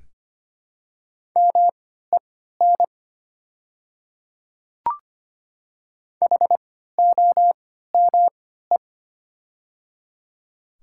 Morse code, heard as a keyed tone, about 25 wpm, spaced as for 14 wpm, 700 Hz. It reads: MEN HOME
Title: 3x Character Spacing - Top 300 Words - 25wpm